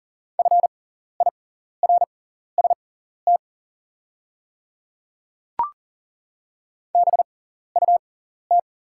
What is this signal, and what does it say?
Morse code, heard as a keyed tone, about 40 wpm, spaced as for 14 wpm, 690 Hz. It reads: FIRST BUT
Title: 6x Character Spacing - Top 100 Words - 40wpm